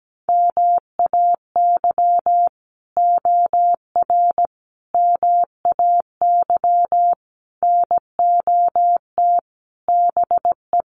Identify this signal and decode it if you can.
Morse code, heard as a keyed tone, about 17 wpm, 700 Hz. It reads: MAY OR MAY NOT BE
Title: Sets of 5 Words 17wpm